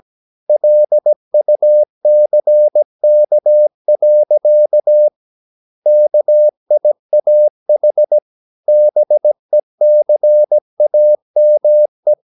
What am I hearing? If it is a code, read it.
Morse code, heard as a keyed tone, about 17 wpm, 600 Hz. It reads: LUCK. KIAH BECAME
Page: Romanen Never af Ken Follet på engelsk.